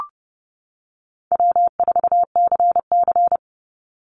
Morse code, heard as a keyed tone, about 30 wpm, 700 Hz. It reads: W4CC
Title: Call Signs - 30wpm